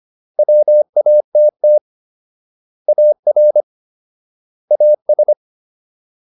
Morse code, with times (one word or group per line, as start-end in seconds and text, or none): 0.4-1.8 WATT
2.9-3.6 AR
4.7-5.3 AS